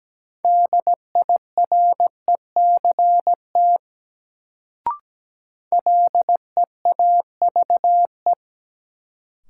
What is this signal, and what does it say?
Morse code, heard as a keyed tone, about 17 wpm, 700 Hz. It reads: DIRECT LEAVE
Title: Sentences from Top 400 Words - New Word Drill - 17wpm